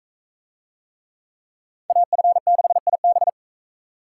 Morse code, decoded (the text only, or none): AF6IB